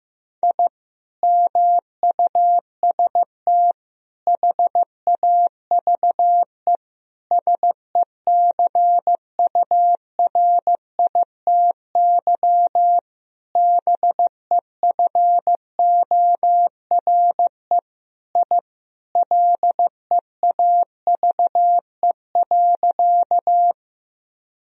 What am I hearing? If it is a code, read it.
Morse code, heard as a keyed tone, about 15 wpm, 700 Hz. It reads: I MUST HAVE SECURITY BEFORE I LEAVE.
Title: Sentences from Top 1000 Words - Review Part 4 - 15wpm